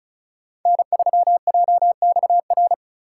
Morse code, heard as a keyed tone, about 34 wpm, 700 Hz. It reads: N3JXR
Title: USA Call Signs - 3x Character Spacing - 35wpm